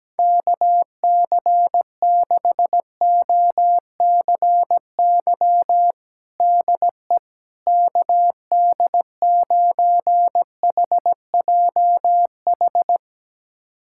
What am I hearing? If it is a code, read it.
Morse code, heard as a keyed tone, about 17 wpm, 700 Hz. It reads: KC6OCY DE KD9HJH
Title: USA Callsign DE Callsign - 6 Characters - 17wpm